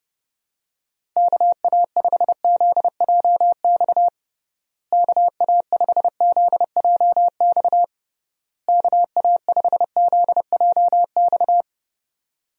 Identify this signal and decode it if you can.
Morse code, heard as a keyed tone, about 32 wpm, 700 Hz. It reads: KA5ZJX KA5ZJX KA5ZJX
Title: USA Call Signs - 6 Characters - Repeated 3x - 30wpm